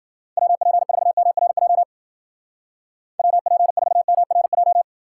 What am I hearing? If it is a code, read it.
Morse code, heard as a keyed tone, about 55 wpm, 700 Hz. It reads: WP4MRJ WP4MRJ